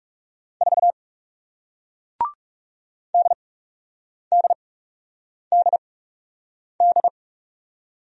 Morse code, heard as a keyed tone, about 40 wpm, spaced as for 8 wpm, 700 Hz. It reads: 4 DDDD